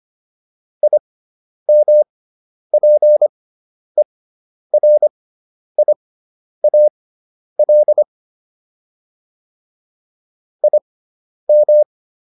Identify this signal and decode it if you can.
Morse code, heard as a keyed tone, about 25 wpm, spaced as for 10 wpm, 600 Hz. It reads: IMPERIAL IM